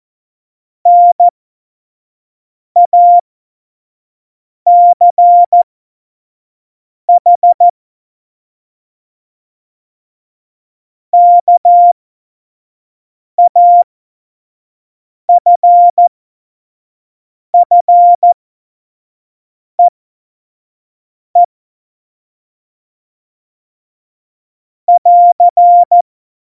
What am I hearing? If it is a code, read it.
Morse code, heard as a keyed tone, about 14 wpm, spaced as for 5 wpm, 700 Hz. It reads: NACH KAFFEE <AR>
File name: Morsekurs / Witze / Witz 22. Oktober 2019, 25 F70